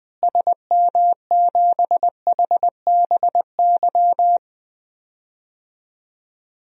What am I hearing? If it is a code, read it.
Morse code, heard as a keyed tone, about 20 wpm, 700 Hz. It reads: SM7HBY
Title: DXCC-100 Call Signs - 6 Characters - Repeated 3x - 20wpm